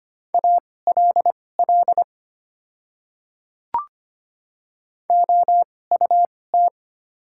Morse code, heard as a keyed tone, about 25 wpm, spaced as for 18 wpm, 700 Hz. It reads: ALL OUT
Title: 2x Character Spacing - Top 300 Words - 25wpm